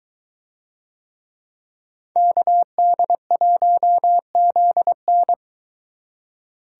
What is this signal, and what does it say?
Morse code, heard as a keyed tone, about 23 wpm, 700 Hz. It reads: KD1ZN